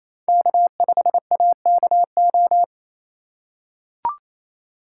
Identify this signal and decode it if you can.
Morse code, heard as a keyed tone, about 28 wpm, 700 Hz. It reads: K5AKO
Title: USA Call Signs - 5 Characters - 2x Character Spacing - 28wpm